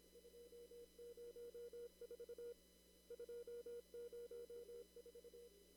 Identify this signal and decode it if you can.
Morse code, heard as a keyed tone, about 25 wpm, 480 Hz. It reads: 204 204
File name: CW / kt5j wpx cw-op-k5pi